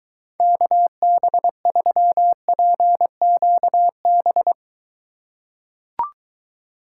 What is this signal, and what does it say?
Morse code, heard as a keyed tone, about 23 wpm, 700 Hz. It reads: KB3PQB